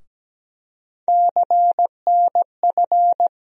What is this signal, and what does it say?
Morse code, heard as a keyed tone, about 17 wpm, 700 Hz. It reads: CNF
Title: USA Call Sign Postfixes - 3 Letters 17wpm